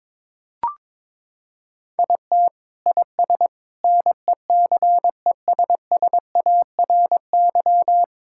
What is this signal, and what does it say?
Morse code, heard as a keyed tone, about 22 wpm, 700 Hz. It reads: IT IS NECESSARY
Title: Sentences from Top 1000 Words - Review Part 5 - 22wpm